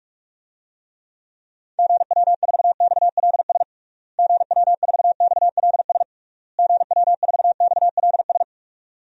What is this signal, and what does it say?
Morse code, heard as a keyed tone, about 41 wpm, 700 Hz. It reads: GW4XLS GW4XLS GW4XLS
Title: DXCC-100 Call Signs - 6 Characters - Repeated 3x - 45wpm